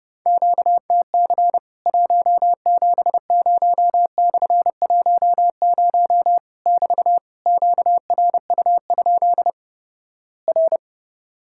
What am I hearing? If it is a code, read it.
Morse code, heard as a keyed tone, about 31 wpm, 700 Hz. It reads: QTC 170/10 <BT> QRU? R
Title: Worked All Europe DX Contest - QTC Exchange - 30wpm